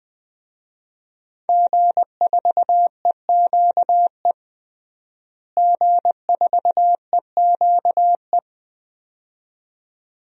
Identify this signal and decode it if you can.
Morse code, heard as a keyed tone, about 20 wpm, 700 Hz. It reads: G4EQE G4EQE
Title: DXCC-100 Call Signs - 5 Characters - Repeated 2x - 20wpm